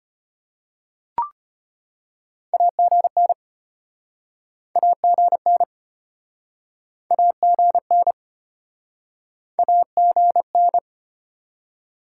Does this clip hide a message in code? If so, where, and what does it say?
Yes: Morse code from 2.5 to 10.8 s, heard as a keyed tone, about 34 wpm, 700 Hz. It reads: AGN AGN AGN AGN